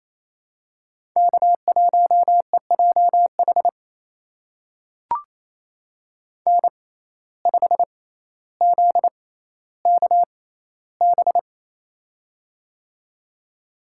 Morse code, heard as a keyed tone, about 28 wpm, spaced as for 14 wpm, 700 Hz. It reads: K1EJH N5ZKB
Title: USA Call Signs - 5 Characters - 6x Character Spacing - 28wpm